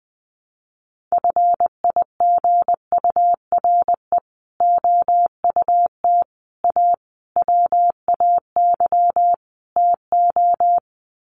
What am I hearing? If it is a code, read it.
Morse code, heard as a keyed tone, about 20 wpm, 700 Hz. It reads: FIGURE OUT A WAY TO
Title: Sets of 5 Words Encore2 20wpm